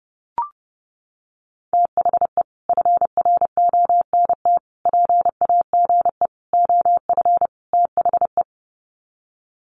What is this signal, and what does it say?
Morse code, heard as a keyed tone, about 30 wpm, 700 Hz. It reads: THE FRONT PAGE OF THE
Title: Sets of 5 Words Encore 30wpm